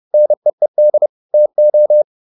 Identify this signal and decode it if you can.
Morse code, heard as a keyed tone, about 30 wpm, 600 Hz. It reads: NEED TO